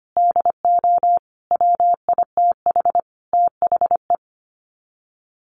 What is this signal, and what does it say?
Morse code, heard as a keyed tone, about 25 wpm, 700 Hz. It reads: DO WITH THE
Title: Sets of 5 Words Encore 25wpm